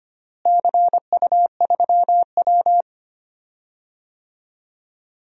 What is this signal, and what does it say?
Morse code, heard as a keyed tone, about 25 wpm, 700 Hz. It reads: CU3W
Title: DXCC-100 Call Signs - 4 Characters - Repeated 3x - 25wpm